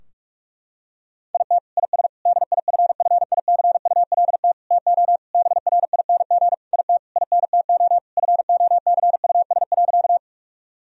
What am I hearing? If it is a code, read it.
Morse code, heard as a keyed tone, about 41 wpm, 700 Hz. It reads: IT IS DIFFICULT TO BRING IT INTO FOCUS.
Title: Sentences from Top 1000 Words - Review Part 4 - 45wpm